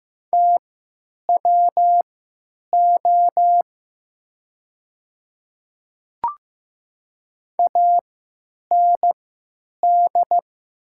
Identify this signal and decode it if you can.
Morse code, heard as a keyed tone, about 15 wpm, spaced as for 9 wpm, 700 Hz. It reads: TWO AND
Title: 3x Character Spacing - Top 100 Words - 15wpm